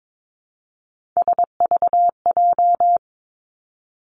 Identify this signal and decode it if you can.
Morse code, heard as a keyed tone, about 22 wpm, 700 Hz. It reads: SVJ
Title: Three Challenging Letter-Number Combinations 22wpm